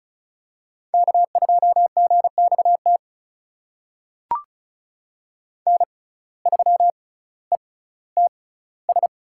Morse code, heard as a keyed tone, about 36 wpm, spaced as for 18 wpm, 700 Hz. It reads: K2GXT N3ETS
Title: USA Call Signs - 5 Characters - 6x Character Spacing - 35wpm